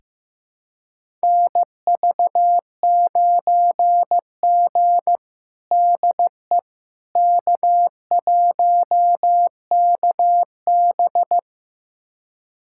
Morse code, heard as a keyed tone, about 15 wpm, 700 Hz. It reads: NV9G DE K1KB
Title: USA Callsign DE Callsign - 4 Characters - 15wpm